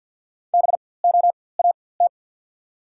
Morse code, heard as a keyed tone, about 54 wpm, spaced as for 24 wpm, 700 Hz. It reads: BOAT
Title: 4x Character Spacing - Top 500 Words - 50wpm